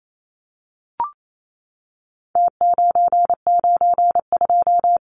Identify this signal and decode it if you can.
Morse code, heard as a keyed tone, about 28 wpm, 700 Hz. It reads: T992
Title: Numbers - Contest Style - 4-Digit 28wpm